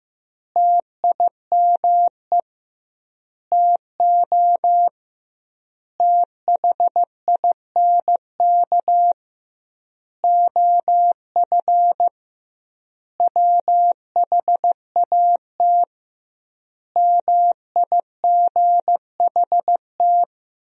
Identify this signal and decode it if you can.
Morse code, heard as a keyed tone, about 16 wpm, 700 Hz. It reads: TIME TO THINK OF WHAT MIGHT
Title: 2x Word-Spacing - Sentences from Top 400 Words - 15wpm